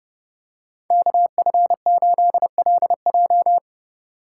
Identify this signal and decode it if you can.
Morse code, heard as a keyed tone, about 29 wpm, 700 Hz. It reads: KF8LJ